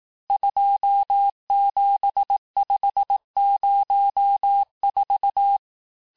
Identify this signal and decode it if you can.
Morse code, heard as a keyed tone, about 18 wpm, 800 Hz. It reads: 27504
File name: cours-du-lundi files / cours-1588-texte-chiffre-vitesse-17MPM